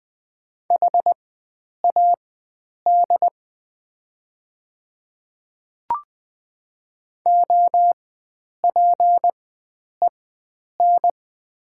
Morse code, heard as a keyed tone, about 20 wpm, spaced as for 9 wpm, 700 Hz. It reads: HAD OPEN
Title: 4x Character Spacing - Top 300 Words - 20wpm